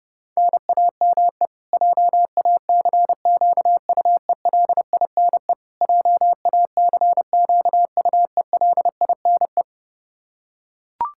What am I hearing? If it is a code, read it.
Morse code, heard as a keyed tone, about 30 wpm, 700 Hz. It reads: NAME JACQUELINE JACQUELINE
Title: K1USN SST QSO Practice 30wpm